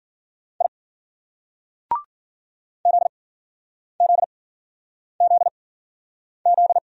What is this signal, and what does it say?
Morse code, heard as a keyed tone, about 54 wpm, spaced as for 9 wpm, 700 Hz. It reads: I ZZZZ